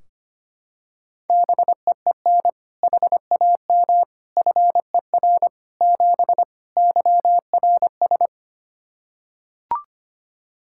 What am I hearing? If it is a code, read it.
Morse code, heard as a keyed tone, about 25 wpm, 700 Hz. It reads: BEEN HAM FER 7 YRS
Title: K1USN SST QSO Practice 25wpm